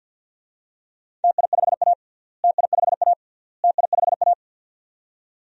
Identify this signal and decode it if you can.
Morse code, heard as a keyed tone, about 50 wpm, 700 Hz. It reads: TI5A TI5A TI5A
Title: DXCC-100 Call Signs - 4 Characters - Repeated 3x - 50wpm